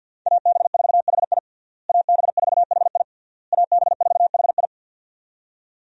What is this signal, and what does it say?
Morse code, heard as a keyed tone, about 53 wpm, 700 Hz. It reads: AB4HI AB4HI AB4HI
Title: USA Call Signs - 5 Characters - Repeated 3x - 50wpm